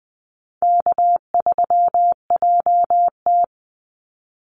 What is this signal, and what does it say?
Morse code, heard as a keyed tone, about 20 wpm, 700 Hz. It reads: K3JT
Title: Easy Call Signs Repeated 2x - 20wpm